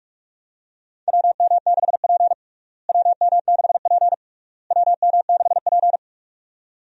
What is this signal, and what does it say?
Morse code, heard as a keyed tone, about 46 wpm, 700 Hz. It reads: WM6P WM6P WM6P
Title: USA Call Signs - 4 Characters - Repeated 3x - 45wpm